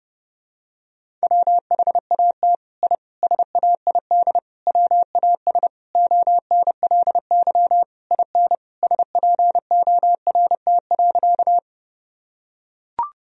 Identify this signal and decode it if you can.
Morse code, heard as a keyed tone, about 30 wpm, 700 Hz. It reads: WHAT I SAID WAS ONLY IN SPORT.
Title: Sentences from Top 1000 Words - Review Part 4 - 30wpm